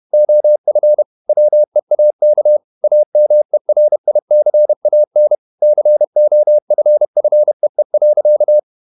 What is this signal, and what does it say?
Morse code, heard as a keyed tone, about 31 wpm, 600 Hz. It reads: OF WEAK AMERICAN COFFEE.
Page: Never 31wpm | CW med Gnister